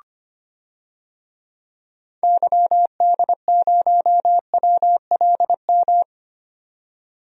Morse code, heard as a keyed tone, about 25 wpm, 700 Hz. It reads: YD0WLM
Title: DXCC-100 Call Signs - 6 Characters - 25wpm